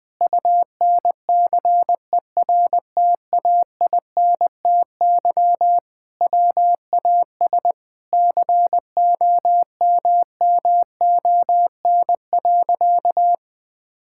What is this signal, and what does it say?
Morse code, heard as a keyed tone, about 20 wpm, 700 Hz. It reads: UNCERTAINTY WAS COMMON.
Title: Sentences with 6-Word Phrases - 20wpm